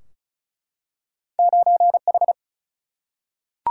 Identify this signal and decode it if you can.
Morse code, heard as a keyed tone, about 36 wpm, 700 Hz. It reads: 9H
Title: Single-Letter Single-Number 35wpm